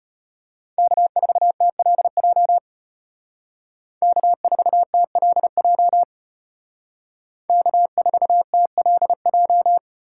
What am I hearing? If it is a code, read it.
Morse code, heard as a keyed tone, about 33 wpm, 700 Hz. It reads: K4TLJ K4TLJ K4TLJ